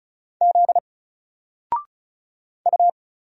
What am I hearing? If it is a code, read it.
Morse code, heard as a keyed tone, about 34 wpm, 700 Hz. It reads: Z U
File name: Single-Letter-Number-35wpm